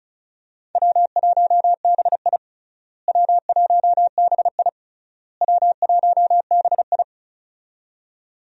Morse code, heard as a keyed tone, about 34 wpm, 700 Hz. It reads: W1BI W1BI W1BI
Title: USA Call Signs - 4 Characters - Repeated 3x - 35wpm